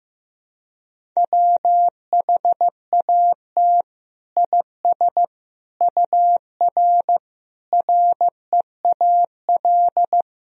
Morse code, heard as a keyed tone, about 15 wpm, 700 Hz. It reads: WHAT IS UR REAL
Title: Sentences from Top 300 Words and Abbreviations 15wpm